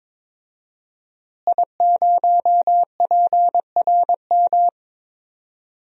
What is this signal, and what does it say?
Morse code, heard as a keyed tone, about 21 wpm, 700 Hz. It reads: I0PRM
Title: DXCC-100 Call Signs - 5 Characters - Repeated 3x - 22wpm